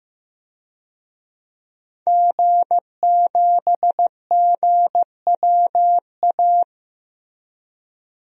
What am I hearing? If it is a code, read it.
Morse code, heard as a keyed tone, about 15 wpm, 700 Hz. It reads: G7GWA
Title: DXCC-100 Call Signs - 5 Characters - Repeated 3x - 15wpm